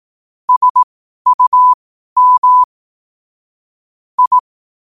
Morse code, heard as a keyed tone, about 18 wpm, spaced as for 12 wpm, 1000 Hz. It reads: SUM I
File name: qotdfiles / 12wpm